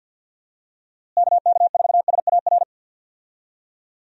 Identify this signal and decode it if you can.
Morse code, heard as a keyed tone, about 52 wpm, 700 Hz. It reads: KK4SAR
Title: USA Call Signs - 6 Characters - Repeated 3x - 50wpm